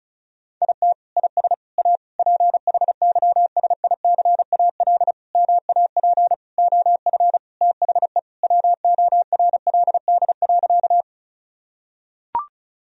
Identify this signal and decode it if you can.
Morse code, heard as a keyed tone, about 34 wpm, 700 Hz. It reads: IT IS A PHYSICAL MAP OF THE WORLD.
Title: Sentences from Top 1000 Words - Review Part 4 - 35wpm